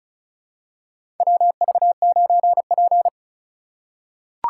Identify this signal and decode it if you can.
Morse code, heard as a keyed tone, about 35 wpm, 700 Hz. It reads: WV9P
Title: USA Call Signs - 4 Characters - 2x Character Spacing - 35wpm